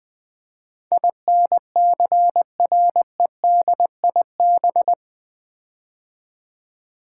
Morse code, heard as a keyed tone, about 20 wpm, 700 Hz. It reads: INCREDIB
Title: Build A Word - 4 Letters At A Time - 10-Letter Words - 20wpm